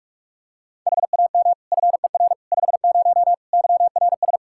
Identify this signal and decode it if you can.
Morse code, heard as a keyed tone, about 46 wpm, 700 Hz. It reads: HAM FER 50 YRS
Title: Practice QSO - Exchange 3 - More Info - 45wpm